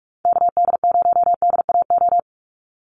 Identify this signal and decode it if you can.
Morse code, heard as a keyed tone, about 44 wpm, 700 Hz. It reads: KD0DAO
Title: USA Call Signs 45wpm